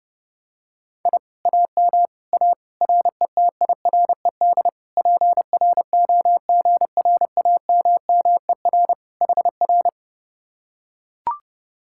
Morse code, heard as a keyed tone, about 29 wpm, 700 Hz. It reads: I AM A RETIRED PROGRAMMER HR